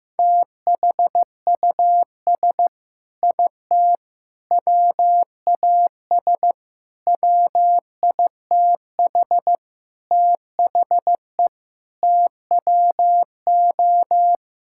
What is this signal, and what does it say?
Morse code, heard as a keyed tone, about 16 wpm, 700 Hz. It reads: THUS IT WAS WITH THE TWO
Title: Sentences from Top 1000 Words - Review Part 5 - 15wpm